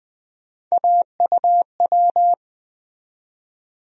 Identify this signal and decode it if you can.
Morse code, heard as a keyed tone, about 20 wpm, 700 Hz. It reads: AUW
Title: USA Call Sign Postfixes - 3 Letters 20wpm